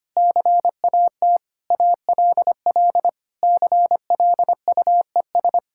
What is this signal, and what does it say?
Morse code, heard as a keyed tone, about 25 wpm, 700 Hz. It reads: CAT ALL CLUES
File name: practice / 25wpm / Sentences with 4-Word Phrases - 25wpm